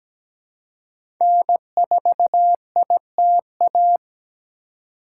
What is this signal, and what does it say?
Morse code, heard as a keyed tone, about 17 wpm, 700 Hz. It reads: N4ITA
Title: USA Call Signs - 5 Characters - 4x Character Spacing - 17wpm